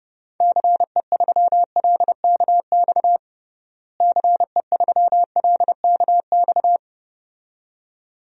Morse code, heard as a keyed tone, about 30 wpm, 700 Hz. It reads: CE3LKX CE3LKX
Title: DXCC-100 Call Signs - 6 Characters - Repeated 2x - 30wpm